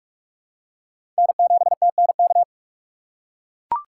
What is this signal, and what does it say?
Morse code, heard as a keyed tone, about 44 wpm, 700 Hz. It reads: N7TNK
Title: USA Call Signs - 5 Characters - 6x Character Spacing - 45wpm